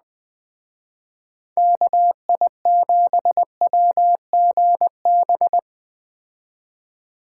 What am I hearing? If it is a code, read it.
Morse code, heard as a keyed tone, about 20 wpm, 700 Hz. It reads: KI7WGB
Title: Speed Racing - USA Call Signs 17wpm